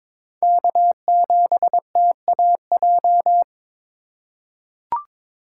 Morse code, heard as a keyed tone, about 22 wpm, 700 Hz. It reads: K7TAJ
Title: USA Call Signs - 5 Characters - 3x Character Spacing - 22wpm